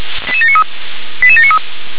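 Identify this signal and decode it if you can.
Five-tone selective call (ZVEI), heard as quick runs of five tones, tones of about 80 ms. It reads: 88793 88793
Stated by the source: Five-tone ZVEI selective call received on an SDR.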